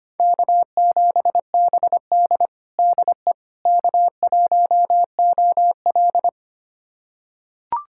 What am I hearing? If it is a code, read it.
Morse code, heard as a keyed tone, about 25 wpm, 700 Hz. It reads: K7BD DE K1OL
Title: USA Callsign DE Callsign - 4 Characters - 25wpm